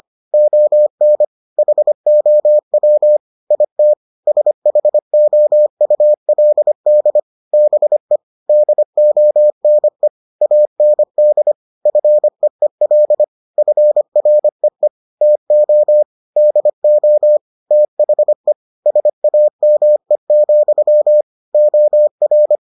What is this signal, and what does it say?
Morse code, heard as a keyed tone, about 25 wpm, 600 Hz. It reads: ON HOW IT SHOULD BE DONE AND FEEL FREE TO DO THE SAME, OR